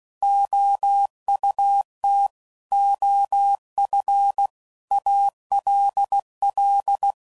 Morse code, heard as a keyed tone, about 16 wpm, 800 Hz. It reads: OUT OF ALL
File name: CW Code Groups @15WPM / Technical Subjects / AND Gate